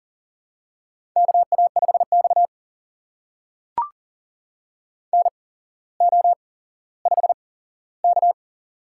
Morse code, heard as a keyed tone, about 38 wpm, spaced as for 17 wpm, 700 Hz. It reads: KA5X NO5K
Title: USA Call Signs - 4 Characters - 8x Character Spacing - 40wpm